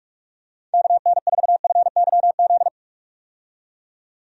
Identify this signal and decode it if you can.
Morse code, heard as a keyed tone, about 46 wpm, 700 Hz. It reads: KN4FYZ
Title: USA Call Signs - 6 Characters - 45wpm